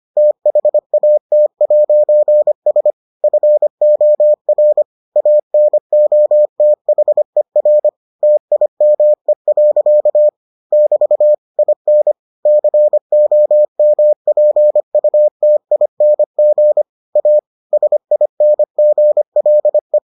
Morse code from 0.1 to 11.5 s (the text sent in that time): THAT'S FOR ANOTHER TIME. <BT>